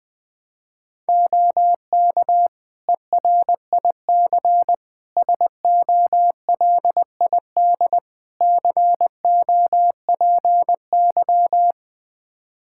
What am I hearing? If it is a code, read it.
Morse code, heard as a keyed tone, about 20 wpm, 700 Hz. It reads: OK ERIC SOLID COPY
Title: Practice QSO - Exchange 3 - More Info - 20wpm